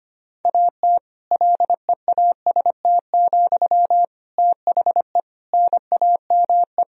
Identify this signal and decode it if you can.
Morse code, heard as a keyed tone, about 25 wpm, 700 Hz. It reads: AT LEAST, THE NAME
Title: Sentences from Top 1000 Words - Review Part 4 - 25wpm